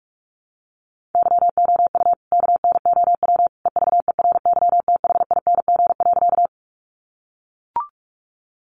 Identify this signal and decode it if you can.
Morse code, heard as a keyed tone, about 44 wpm, 700 Hz. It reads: YOU KNOW EVERYTHING.
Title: Sentences from Top 700 Words 45wpm